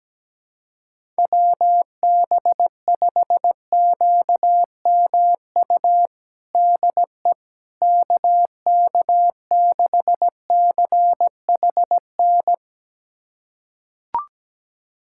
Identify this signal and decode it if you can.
Morse code, heard as a keyed tone, about 17 wpm, 700 Hz. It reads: WB5QMU DE KK6CHN
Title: USA Callsign DE Callsign 17wpm